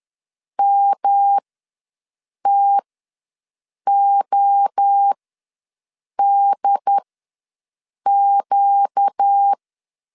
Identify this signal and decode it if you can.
Morse code, heard as a keyed tone, about 10 wpm, spaced as for 6 wpm, 800 Hz. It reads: MTODQ
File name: LEZIONI / 6 PKX0 / 6 PKX0 ws 10wpm 4220